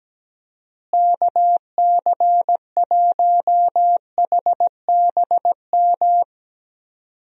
Morse code, heard as a keyed tone, about 17 wpm, 700 Hz. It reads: KC1HBM
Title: USA Call Signs - 6 Characters - Repeated 3x - 17wpm